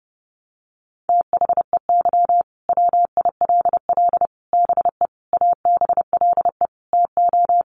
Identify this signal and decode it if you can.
Morse code, heard as a keyed tone, about 30 wpm, 700 Hz. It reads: THEY WILL BE ABLE TO
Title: Sets of 5 Words Encore 30wpm